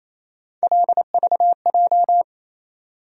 Morse code, heard as a keyed tone, about 28 wpm, 700 Hz. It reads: LVJ
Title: USA Call Sign Postfixes - 3 Letters 28wpm